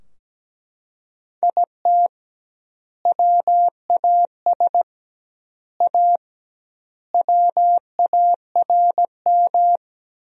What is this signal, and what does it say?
Morse code, heard as a keyed tone, about 17 wpm, 700 Hz. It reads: IT WAS A WARM
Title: 2x Word-Spacing - Sentences from Top 500 Words - 17wpm